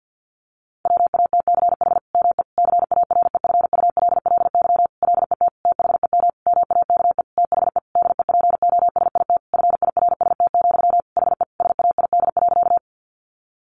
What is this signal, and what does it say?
Morse code, heard as a keyed tone, about 50 wpm, 700 Hz. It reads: WATCH ME CAREFULLY LET THEM MAKE THE DEPOSIT FIRST, HE SAID.
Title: Sentences from Top 1300 Words 50wpm